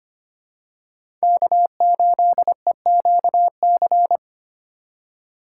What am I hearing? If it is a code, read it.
Morse code, heard as a keyed tone, about 25 wpm, 700 Hz. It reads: K8EQC
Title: USA Call Signs - 5 Characters - 6x Character Spacing - 25wpm